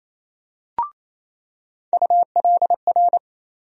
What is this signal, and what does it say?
Morse code, heard as a keyed tone, about 28 wpm, 700 Hz. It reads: ULR
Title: USA Call Sign Postfixes - 3 Letters 28wpm